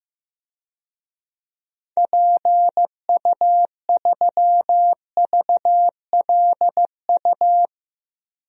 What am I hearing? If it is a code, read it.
Morse code, heard as a keyed tone, about 15 wpm, 700 Hz. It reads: PU3VLU
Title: DXCC-100 Call Signs - 6 Characters - Repeated 2x - 15wpm